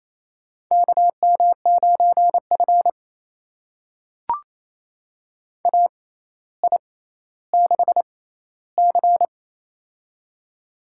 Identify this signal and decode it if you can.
Morse code, heard as a keyed tone, about 28 wpm, spaced as for 14 wpm, 700 Hz. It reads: KM9F AI6C
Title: USA Call Signs - 4 Characters - 6x Character Spacing - 28wpm